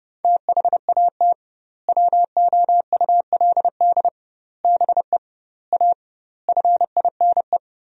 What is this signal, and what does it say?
Morse code, heard as a keyed tone, about 32 wpm, 700 Hz. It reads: THAT WOULD BE A FINE
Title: 2x Word-Spacing - Sentences from Top 500 Words - 30wpm